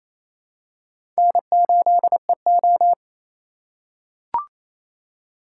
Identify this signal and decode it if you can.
Morse code, heard as a keyed tone, about 27 wpm, 700 Hz. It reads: N8EO
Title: USA Call Signs - 4 Characters - 3x Character Spacing - 28wpm